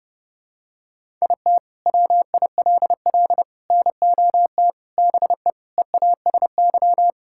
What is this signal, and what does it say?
Morse code, heard as a keyed tone, about 32 wpm, 700 Hz. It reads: IT WILL NOT BE EASY
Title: Sentences from Top 1000 Words - Review Part 5 - 30wpm